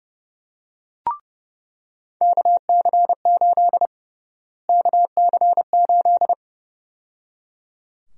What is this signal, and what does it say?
Morse code, heard as a keyed tone, about 31 wpm, 700 Hz. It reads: KC8 KC8
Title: USA Call Sign Prefixes - Repeated 2x - 30wpm